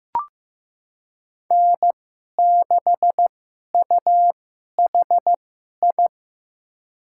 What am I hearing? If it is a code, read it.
Morse code, heard as a keyed tone, about 15 wpm, spaced as for 11 wpm, 700 Hz. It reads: N6UHI
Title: USA Call Signs - 5 Characters - 2x Character Spacing - 15wpm